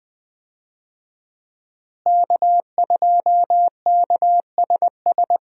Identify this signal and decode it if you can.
Morse code, heard as a keyed tone, about 20 wpm, 700 Hz. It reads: K2KSS
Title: USA Call Signs - 5 Characters - Repeated 2x - 20wpm